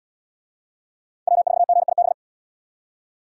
Morse code, heard as a keyed tone, about 77 wpm, 700 Hz. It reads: W4GIB